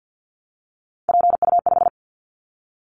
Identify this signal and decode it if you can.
Morse code, heard as a keyed tone, about 50 wpm, 700 Hz. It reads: PU5
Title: Callsign Prefixes for Top 100 DXCC - 50wpm